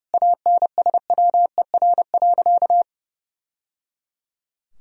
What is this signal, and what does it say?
Morse code, heard as a keyed tone, about 30 wpm, 700 Hz. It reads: ANSWER.
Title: Sentences from Top 1000 Words - Review Part 4 - 30wpm